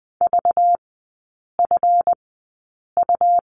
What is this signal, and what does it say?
Morse code, heard as a keyed tone, about 20 wpm, spaced as for 8 wpm, 700 Hz. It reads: VFU
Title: 191 Lesson 31e - Letter v - Three Characters 20wpm